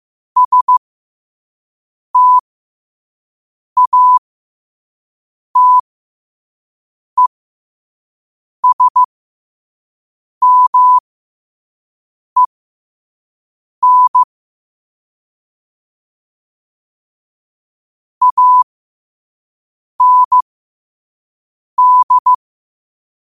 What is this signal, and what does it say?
Morse code, heard as a keyed tone, about 15 wpm, spaced as for 5 wpm, 1000 Hz. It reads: STATESMEN AND
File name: qotdfiles / 5wpm